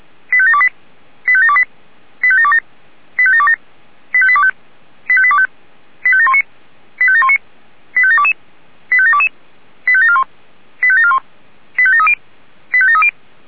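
Five-tone selective call (ZVEI), heard as quick runs of five tones, tones of about 70 ms. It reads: 85728 85728 85727 85727 85825 85825 85719 85719 85722 85722 85741 85741 85730 85730
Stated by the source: ZVEI five-tone selective call received by SDR.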